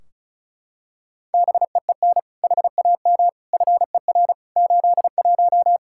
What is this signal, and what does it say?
Morse code, heard as a keyed tone, about 36 wpm, 700 Hz. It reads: BEEN HAM FER 81
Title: Practice QSO - Exchange 3 - More Info - 35wpm